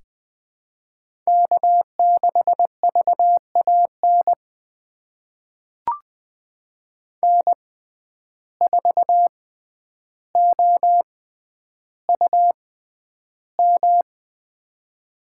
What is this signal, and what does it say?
Morse code, heard as a keyed tone, about 21 wpm, spaced as for 10 wpm, 700 Hz. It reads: K6VAN N4OUM
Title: USA Call Signs - 5 Characters - 6x Character Spacing - 20wpm